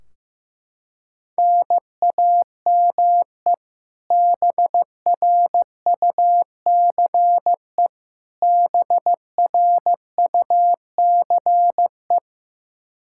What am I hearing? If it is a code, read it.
Morse code, heard as a keyed tone, about 15 wpm, 700 Hz. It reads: NAME BRUCE BRUCE
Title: Practice QSO - Exchange 1 - Beginning - 15wpm